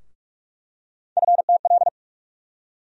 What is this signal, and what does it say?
Morse code, heard as a keyed tone, about 43 wpm, 700 Hz. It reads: FTL
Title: USA Call Sign Postfixes - 3 Letters 45wpm